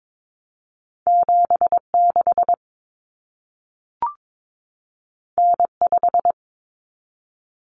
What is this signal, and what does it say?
Morse code, heard as a keyed tone, about 22 wpm, 700 Hz. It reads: 76 N5
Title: Common Cut Numbers - 2-Digit 22wpm